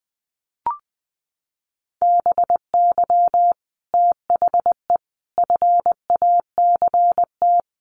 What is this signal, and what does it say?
Morse code, heard as a keyed tone, about 20 wpm, 700 Hz. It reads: BY THE FACT
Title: Sets of 4 Words 20wpm